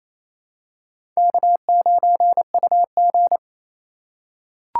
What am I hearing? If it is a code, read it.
Morse code, heard as a keyed tone, about 28 wpm, 700 Hz. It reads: K9UG
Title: USA Call Signs - 4 Characters - 2x Character Spacing - 28wpm